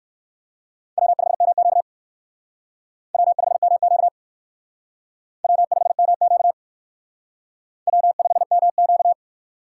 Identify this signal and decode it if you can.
Morse code, heard as a keyed tone, about 56 wpm, 700 Hz. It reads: W5MQ W5MQ W5MQ W5MQ
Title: Speed Racing - USA Call Signs 45wpm